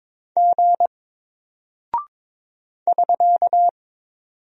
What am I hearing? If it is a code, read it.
Morse code, heard as a keyed tone, about 22 wpm, 700 Hz. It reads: G <SK>
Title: Single Character (Letter:Number:Punctuation:Prosign) Alt 22wpm